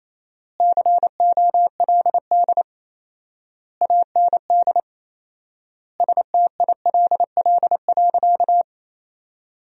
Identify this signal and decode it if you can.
Morse code, heard as a keyed tone, about 28 wpm, 700 Hz. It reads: COLD AND STILL.
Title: 4x Word-Spacing - Sentences from Top 400 Words - 28wpm